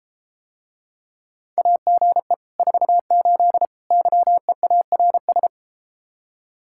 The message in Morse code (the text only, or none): AGE 48 YEARS